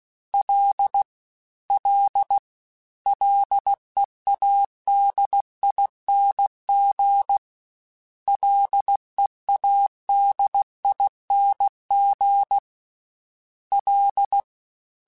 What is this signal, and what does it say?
Morse code, heard as a keyed tone, about 16 wpm, 800 Hz. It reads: L L LEADING LEADING L
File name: Ls / Single L m3s / Single L Prefixe 3x2 Twice @15